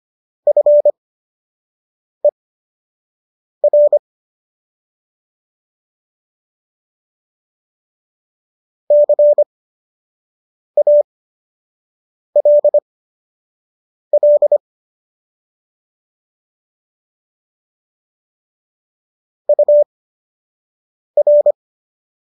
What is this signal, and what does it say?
Morse code, heard as a keyed tone, about 25 wpm, spaced as for 6 wpm, 590 Hz. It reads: FER CALL UR
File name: qso012 06